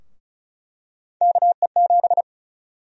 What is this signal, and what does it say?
Morse code, heard as a keyed tone, about 36 wpm, 700 Hz. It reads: KE7
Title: USA Call Sign Prefixes - Repeated 2x - 35wpm